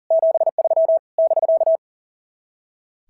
Morse code, heard as a keyed tone, about 40 wpm, 690 Hz. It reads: 73 <BK>
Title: SOTA QSO Practice 40wpm